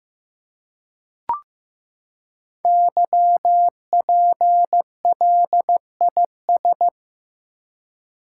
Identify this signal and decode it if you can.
Morse code, heard as a keyed tone, about 15 wpm, 700 Hz. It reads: YPLIS
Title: Code Groups - 5 Letters - 15wpm